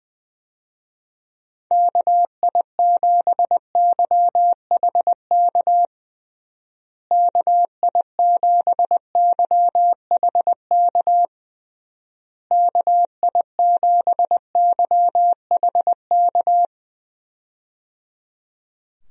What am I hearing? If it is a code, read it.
Morse code, heard as a keyed tone, about 20 wpm, 700 Hz. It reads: KI7YHK KI7YHK KI7YHK
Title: USA Call Signs - 6 Characters - Repeated 3x - 20wpm